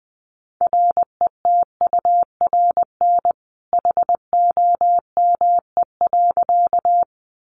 Morse code, heard as a keyed tone, about 20 wpm, 700 Hz. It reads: RETURN HOME.
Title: Sentences from Top 800 Words 20wpm